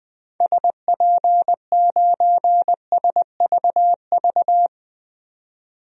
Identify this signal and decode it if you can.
Morse code, heard as a keyed tone, about 21 wpm, 700 Hz. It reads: SP9SVV
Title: DXCC-100 Call Signs - 6 Characters - 20wpm